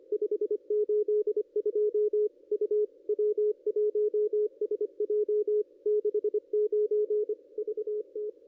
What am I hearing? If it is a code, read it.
Morse code, heard as a keyed tone, about 26 wpm, 440 Hz, with radio noise. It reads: 582 U W1SJ 69 VT
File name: contests / ss-cw-k5tr-op-n5rz -0500